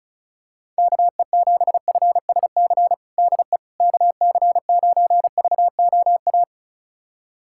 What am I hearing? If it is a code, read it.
Morse code, heard as a keyed tone, about 34 wpm, 690 Hz. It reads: KE7FSC DE KC9VOA
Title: USA Callsign DE Callsign - 6 Characters - 35wpm